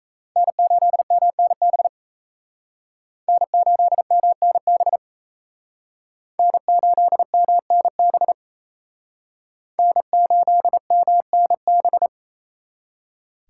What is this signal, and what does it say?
Morse code, heard as a keyed tone, about 35 wpm, 700 Hz. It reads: N8MNB N8MNB N8MNB N8MNB
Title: Speed Racing - USA 1x3 Call Signs 28wpm